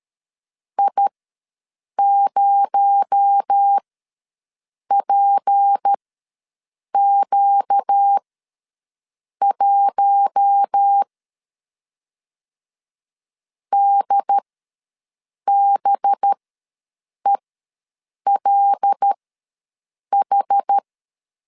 Morse code, heard as a keyed tone, about 13 wpm, spaced as for 6 wpm, 800 Hz. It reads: I0PQ1 DBELH
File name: LEZIONI / 6 PKX0 / 6 PKX0 ws 12wpm 3448